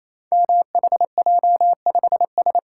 Morse code, heard as a keyed tone, about 27 wpm, 700 Hz. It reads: MHJ5S
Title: Code Groups - 5 Letter:Numbers - 28wpm